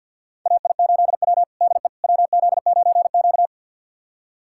Morse code, heard as a keyed tone, about 51 wpm, 690 Hz. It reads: AI8W DE WZ9Q
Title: USA Callsign DE Callsign - 4 Characters - 50wpm